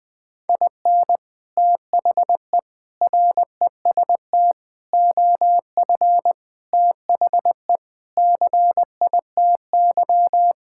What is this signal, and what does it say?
Morse code, heard as a keyed tone, about 20 wpm, 700 Hz. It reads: IN THE REST OF THE CITY
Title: Sentences with 5-Word Phrases - 20wpm